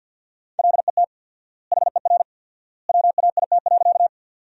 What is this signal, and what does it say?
Morse code, heard as a keyed tone, about 50 wpm, 700 Hz. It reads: LET HER WAIT.
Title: 4x Word-Spacing - Sentences from Top 500 Words - 50wpm